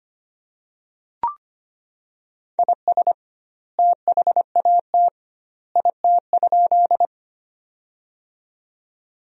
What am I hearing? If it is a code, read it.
Morse code, heard as a keyed tone, about 25 wpm, 700 Hz. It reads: IS THAT IT?
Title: 2x Word-Spacing - Sentences from Top 100 Words - 25wpm